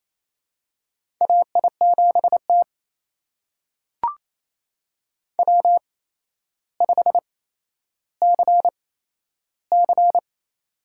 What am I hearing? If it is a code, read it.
Morse code, heard as a keyed tone, about 28 wpm, spaced as for 12 wpm, 700 Hz. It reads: AI7T W5CC
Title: USA Call Signs - 4 Characters - 8x Character Spacing - 28wpm